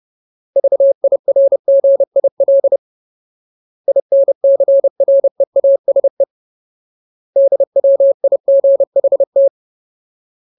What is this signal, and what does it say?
Morse code, heard as a keyed tone, about 30 wpm, 550 Hz. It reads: VIRGIL INCREASE DWIGHT
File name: Ultimate Word List / 30WPM Ultimate